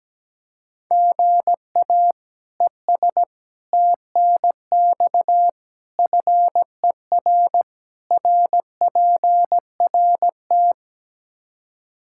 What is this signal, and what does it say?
Morse code, heard as a keyed tone, about 17 wpm, 700 Hz. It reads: GA ES TNX FER RPRT
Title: Practice QSO - Exchange 1 - Beginning - 17wpm